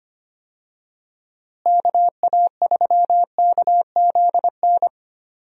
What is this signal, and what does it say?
Morse code, heard as a keyed tone, about 25 wpm, 700 Hz. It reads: KA3KZN
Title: USA Call Signs - 6 Characters - Repeated 3x - 25wpm